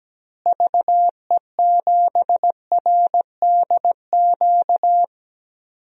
Morse code, heard as a keyed tone, about 17 wpm, 700 Hz. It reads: VE7RDQ
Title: DXCC-100 Call Signs - 6 Characters - Repeated 3x - 17wpm